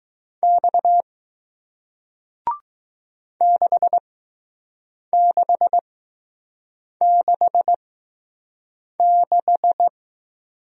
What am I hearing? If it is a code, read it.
Morse code, heard as a keyed tone, about 20 wpm, spaced as for 6 wpm, 700 Hz. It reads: X 6666